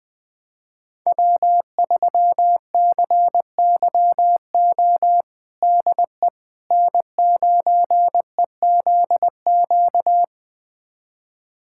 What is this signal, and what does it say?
Morse code, heard as a keyed tone, about 20 wpm, 700 Hz. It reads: W3CYO DE N9EZQ
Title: USA Callsign DE Callsign - 5 Characters - 20wpm